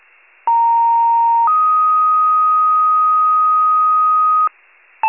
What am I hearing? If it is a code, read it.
Two-tone page, heard as a one-second tone, then a three-second tone, about 930 then 1250 Hz. It is heard from 0.5 to 4.5 s.